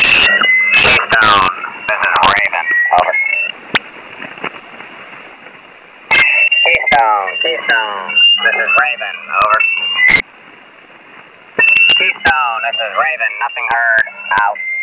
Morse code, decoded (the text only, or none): none